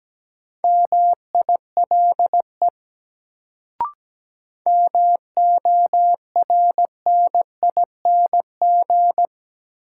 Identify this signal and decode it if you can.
Morse code, heard as a keyed tone, about 17 wpm, 700 Hz. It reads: MILE MORNING
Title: Sentences from Top 400 Words - New Word Drill - 17wpm